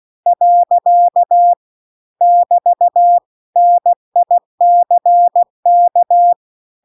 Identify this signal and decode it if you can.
Morse code, heard as a keyed tone, about 16 wpm, 700 Hz. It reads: . <BT> NICK